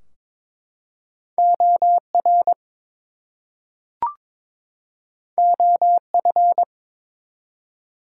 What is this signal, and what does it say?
Morse code, heard as a keyed tone, about 23 wpm, 700 Hz. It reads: OR OF
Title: Two Letter Words - 22wpm